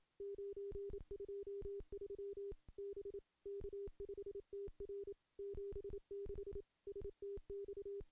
Morse code, heard as a keyed tone, about 27 wpm, 400 Hz. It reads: 923 B K5TR 76 STX
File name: contests / CW-SS-k5tr- -1400